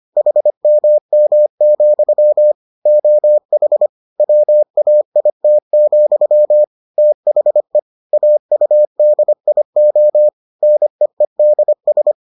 Morse code, heard as a keyed tone, about 25 wpm, 600 Hz. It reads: HMM, OH WAIT, THE AUDIO NEEDS